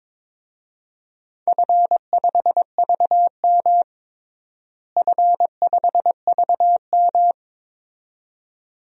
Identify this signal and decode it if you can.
Morse code, heard as a keyed tone, about 22 wpm, 700 Hz. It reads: F5VM F5VM
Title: DXCC-100 Call Signs - 4 Characters - Repeated 2x - 22wpm